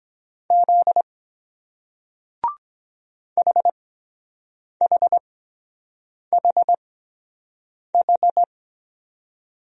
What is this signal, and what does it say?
Morse code, heard as a keyed tone, about 24 wpm, spaced as for 7 wpm, 700 Hz. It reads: Z HHHH